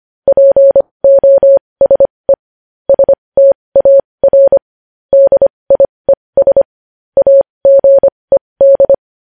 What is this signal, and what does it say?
Morse code, heard as a keyed tone, about 25 wpm, spaced as for 20 wpm, 550 Hz. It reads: POSE STAR DIES AGED
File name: podcast / News headlines-1x-25x20